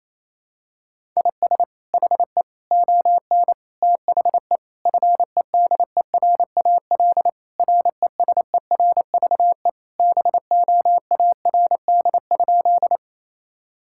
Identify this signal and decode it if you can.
Morse code, heard as a keyed tone, about 28 wpm, 700 Hz. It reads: IS HE ON THE FEDERAL RESERVE BOARD?
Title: Sentences from Top 1000 Words - Review Part 6 - 28wpm